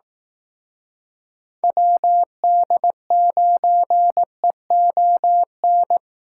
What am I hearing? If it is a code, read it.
Morse code, heard as a keyed tone, about 18 wpm, 700 Hz. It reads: WD9EON
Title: Speed Racing - USA Call Signs 15wpm